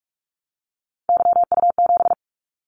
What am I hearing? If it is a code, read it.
Morse code, heard as a keyed tone, about 46 wpm, 700 Hz. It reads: YU7
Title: Callsign Prefixes for Top 25 DXCC - 45wpm